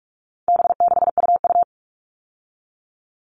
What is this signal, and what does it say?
Morse code, heard as a keyed tone, about 44 wpm, 700 Hz. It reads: B6UU
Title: All Cut Numbers - 4-Digit 45wpm